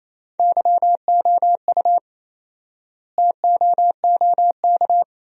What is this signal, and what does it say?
Morse code, heard as a keyed tone, about 28 wpm, 700 Hz. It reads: YOU TOOK
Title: 4x Word-Spacing - Sentences from Top 300 Words - 28wpm